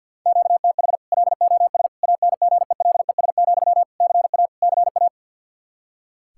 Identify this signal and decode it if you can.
Morse code, heard as a keyed tone, about 46 wpm, 700 Hz. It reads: QTH LOS ANGELES, CA CA
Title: Practice QSO - Exchange 1 - Beginning - 50wpm